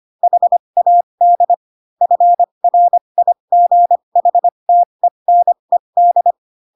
Morse code, heard as a keyed tone, about 25 wpm, spaced as for 22 wpm, 700 Hz. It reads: HAD FRIGHTENED
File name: Pinocchio / Pinocchio ch2 18 25 ik1hge